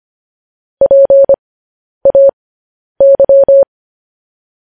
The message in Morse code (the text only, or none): PAY